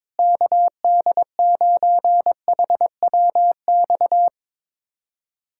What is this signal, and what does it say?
Morse code, heard as a keyed tone, about 22 wpm, 700 Hz. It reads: KD9HWX
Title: USA Callsign DE Callsign - 6 Characters - 22wpm